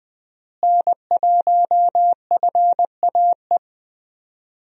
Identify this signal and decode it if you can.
Morse code, heard as a keyed tone, about 20 wpm, 700 Hz. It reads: N1FAE
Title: USA Call Signs - 5 Characters - Repeated 3x - 20wpm